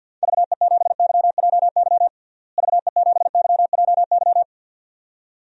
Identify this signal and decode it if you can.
Morse code, heard as a keyed tone, about 52 wpm, 700 Hz. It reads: VE7YJY VE7YJY
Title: DXCC-100 Call Signs - 6 Characters - Repeated 2x - 50wpm